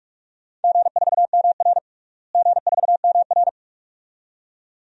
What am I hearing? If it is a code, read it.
Morse code, heard as a keyed tone, about 46 wpm, 700 Hz. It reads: G4MR G4MR
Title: DXCC-100 Call Signs - 4 Characters - Repeated 2x - 45wpm